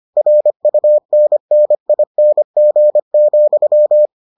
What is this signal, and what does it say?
Morse code, heard as a keyed tone, about 25 wpm, 600 Hz. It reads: RUNNING,